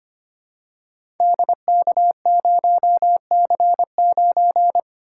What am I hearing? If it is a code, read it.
Morse code, heard as a keyed tone, about 25 wpm, 700 Hz. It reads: DK0C9
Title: Code Groups - 5 Letter:Numbers - 25wpm